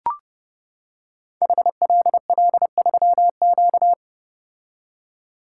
Morse code, heard as a keyed tone, about 30 wpm, 700 Hz. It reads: HLL3Q
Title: Code Groups - 5 Letter:Numbers - 30wpm